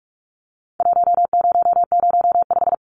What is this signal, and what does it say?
Morse code, heard as a keyed tone, about 46 wpm, 700 Hz. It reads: 1005
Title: Numbers - Contest Style - 4-Digit 45wpm